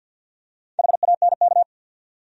Morse code, heard as a keyed tone, about 52 wpm, 700 Hz. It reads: HANK
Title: USA Common Names - Max 5 Letters - 50wpm